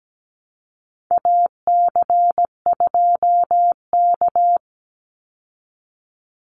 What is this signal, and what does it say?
Morse code, heard as a keyed tone, about 17 wpm, 700 Hz. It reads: AC2K
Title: Call Signs Repeated 2x - 17wpm